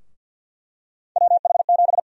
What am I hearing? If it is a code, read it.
Morse code, heard as a keyed tone, about 52 wpm, 700 Hz. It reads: WH7
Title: USA Call Sign Prefixes - Repeated 2x - 50wpm